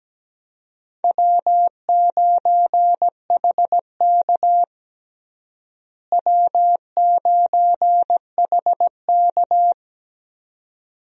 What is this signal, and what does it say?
Morse code, heard as a keyed tone, about 17 wpm, 700 Hz. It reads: W9HK W9HK
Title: USA Call Signs - 4 Characters - Repeated 2x - 17wpm